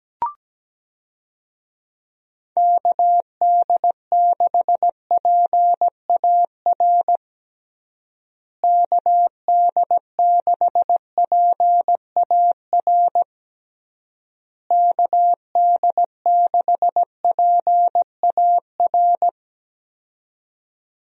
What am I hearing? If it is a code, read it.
Morse code, heard as a keyed tone, about 17 wpm, 700 Hz. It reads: KD6PAR KD6PAR KD6PAR
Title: USA Call Signs - 6 Characters - Repeated 3x - 17wpm